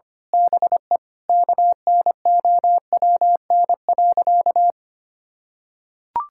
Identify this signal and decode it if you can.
Morse code, heard as a keyed tone, about 25 wpm, 700 Hz. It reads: BE KNOWN.
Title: Sentences from Top 1000 Words - Review Part 6 - 25wpm